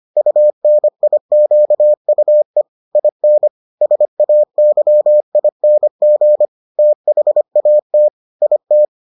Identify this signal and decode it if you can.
Morse code, heard as a keyed tone, about 25 wpm, 600 Hz. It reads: UNIQUE IN SAYING THAT IT